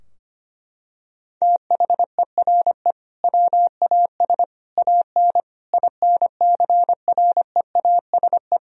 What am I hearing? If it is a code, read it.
Morse code, heard as a keyed tone, about 25 wpm, 700 Hz. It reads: THERE WAS AN INCREASE
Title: Sentences from Top 1000 Words - Review Part 4 - 25wpm